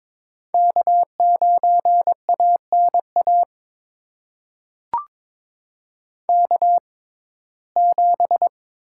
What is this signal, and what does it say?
Morse code, heard as a keyed tone, about 22 wpm, spaced as for 16 wpm, 700 Hz. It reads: K9ANA K7
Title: USA Call Signs - 5 Characters - 6x Character Spacing - 22wpm